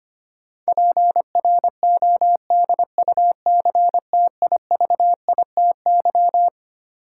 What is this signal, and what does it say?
Morse code, heard as a keyed tone, about 25 wpm, 700 Hz. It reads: PRODUCTIVITY
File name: Twelve Letter Words - 25wpm